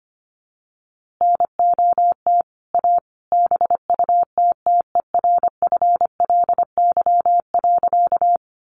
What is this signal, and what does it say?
Morse code, heard as a keyed tone, about 25 wpm, 700 Hz. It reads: NOT A BUTTERFLY.
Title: A Sound Of Thunder 25wpm